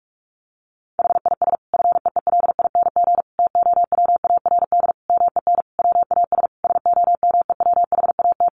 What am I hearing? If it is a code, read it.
Morse code, heard as a keyed tone, about 44 wpm, 700 Hz. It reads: HIS FEELING TOWARD MEN WAS SOMEWHAT
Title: Sentences from Top 1300 Words 45wpm